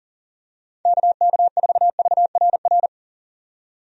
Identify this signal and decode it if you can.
Morse code, heard as a keyed tone, about 40 wpm, 700 Hz. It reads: KK4VRR
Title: USA Call Signs - 6 Characters - 40wpm